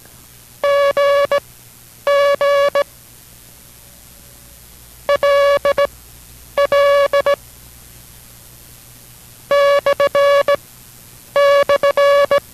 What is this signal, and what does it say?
Morse code, heard as a keyed tone, about 17 wpm, spaced as for 9 wpm, 570 Hz. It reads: GG LL //